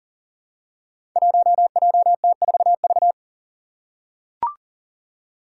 Morse code, heard as a keyed tone, about 43 wpm, 700 Hz. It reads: 1JT4V
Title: Five Challenging Letter-Number Combinations 40wpm